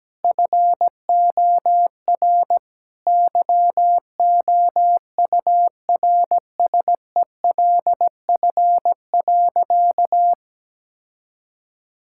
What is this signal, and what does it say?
Morse code, heard as a keyed tone, about 17 wpm, 700 Hz. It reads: FOR YOURSELF.
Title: Sentences from Top 1000 Words - Review Part 6 - 17wpm